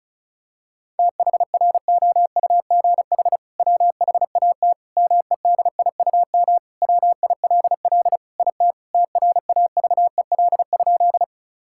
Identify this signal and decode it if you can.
Morse code, heard as a keyed tone, about 34 wpm, 700 Hz. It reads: THROUGH WHAT MEDIUM WILL IT TRAVEL?
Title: Sentences from Top 1000 Words - Review Part 6 - 35wpm